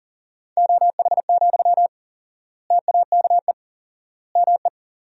Morse code, heard as a keyed tone, about 40 wpm, 700 Hz. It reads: OH, TAKE ME
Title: 4x Word-Spacing - Sentences from Top 500 Words - 40wpm